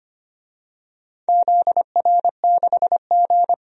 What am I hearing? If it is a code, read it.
Morse code, heard as a keyed tone, about 25 wpm, 700 Hz. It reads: ZR6G
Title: DXCC-100 Call Signs - 4 Characters - 25wpm